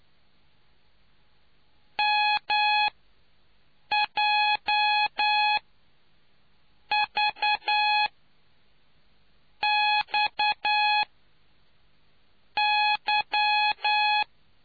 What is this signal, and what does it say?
Morse code, heard as a keyed tone, about 9 wpm, spaced as for 5 wpm, 840 Hz. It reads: MJVXY